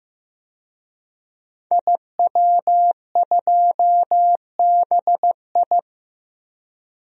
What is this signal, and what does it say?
Morse code, heard as a keyed tone, about 15 wpm, 700 Hz. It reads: IW2BI
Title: DXCC-100 Call Signs - 5 Characters - 15wpm